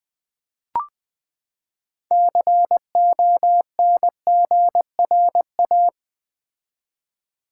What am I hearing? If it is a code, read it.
Morse code, heard as a keyed tone, about 20 wpm, 700 Hz. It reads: CONGRA
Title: Build A Word - 6 Letters At A Time - 12-Letter Words - 20wpm